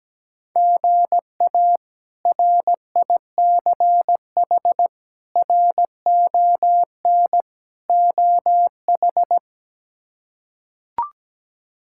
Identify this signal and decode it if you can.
Morse code, heard as a keyed tone, about 17 wpm, 700 Hz. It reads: GA RICH RON OH
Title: K1USN SST QSO Practice - 17wpm